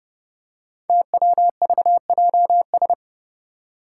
Morse code, heard as a keyed tone, about 28 wpm, 700 Hz. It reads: TWVJS
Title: Five Challenging Letter-Number Combinations 30wpm